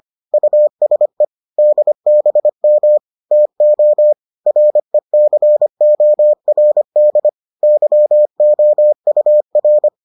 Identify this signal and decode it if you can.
Morse code, heard as a keyed tone, about 25 wpm, 600 Hz. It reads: USE DBM TO RECORD YOUR